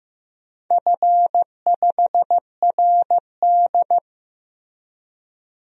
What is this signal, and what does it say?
Morse code, heard as a keyed tone, about 15 wpm, 700 Hz. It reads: F5RD
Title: DXCC-100 Call Signs - 4 Characters - Repeated 2x - 15wpm